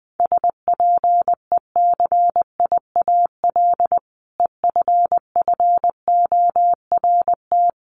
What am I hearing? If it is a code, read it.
Morse code, heard as a keyed tone, about 20 wpm, 700 Hz. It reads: SPECIAL EFFORT
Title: Sentences from Top 1200 Words 20wpm